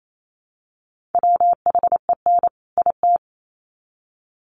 Morse code, heard as a keyed tone, about 28 wpm, 700 Hz. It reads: WHEN IT
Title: Sets of 2 Words 28wpm